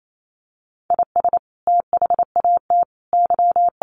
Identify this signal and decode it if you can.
Morse code, heard as a keyed tone, about 28 wpm, 700 Hz. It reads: IS THAT Y
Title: Sentences from Top 1200 Words 28wpm